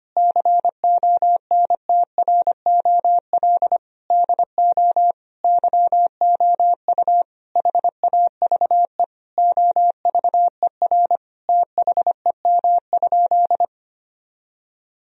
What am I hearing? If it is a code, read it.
Morse code, heard as a keyed tone, about 25 wpm, 700 Hz. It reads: CONTROL DO YOU HAVE OVER THEM?
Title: Sentences from Top 1000 Words - Review Part 3 - 25wpm